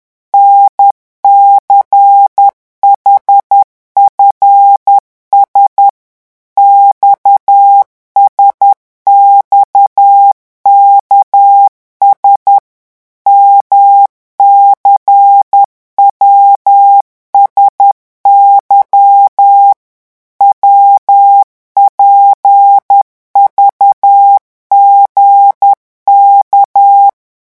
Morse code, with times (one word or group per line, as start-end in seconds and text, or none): 0.3-5.9 NCHFS
6.6-12.6 XSXKS
13.3-19.7 MCWSY
20.4-27.1 WPVGK